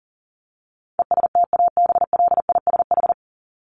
Morse code, heard as a keyed tone, about 40 wpm, 700 Hz. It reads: ESTABLISH
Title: ED Suffix - Top 100 Words with Base Words 40wpm